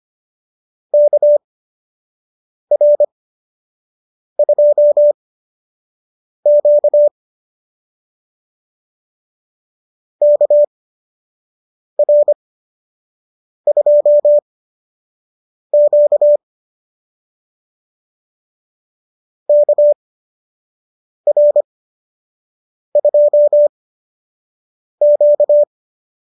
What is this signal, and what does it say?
Morse code, heard as a keyed tone, about 25 wpm, spaced as for 6 wpm, 590 Hz. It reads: KR2Q KR2Q KR2Q